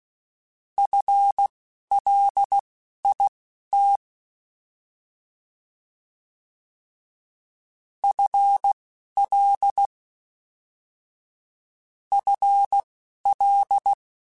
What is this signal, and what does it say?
Morse code, heard as a keyed tone, about 16 wpm, spaced as for 11 wpm, 800 Hz. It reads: FLIT FL FL